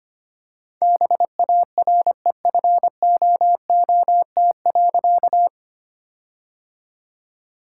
Morse code, heard as a keyed tone, about 25 wpm, 700 Hz. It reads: BAREFOOT.
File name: practice / 25wpm / All-Age Appropriate Jokes - 25wpm